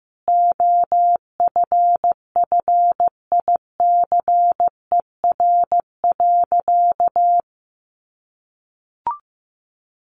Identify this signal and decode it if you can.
Morse code, heard as a keyed tone, about 15 wpm, 700 Hz. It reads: OFFICER.
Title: Sentences from Top 1200 Words 15wpm